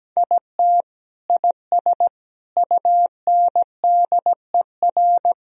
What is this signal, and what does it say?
Morse code, heard as a keyed tone, about 17 wpm, 700 Hz. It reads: IT IS UNDER
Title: Sentences from Top 1000 Words - Review Part 1 - 17wpm